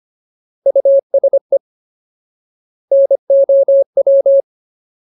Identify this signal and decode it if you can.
Morse code, heard as a keyed tone, about 25 wpm, 550 Hz. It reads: USE NOW
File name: Ultimate Word List / 25WPM Ultimate